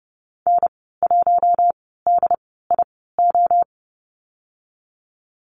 Morse code, heard as a keyed tone, about 30 wpm, spaced as for 17 wpm, 700 Hz. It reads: N1DIO
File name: Lesson 14g - Number 2 - Call Signs 30wpm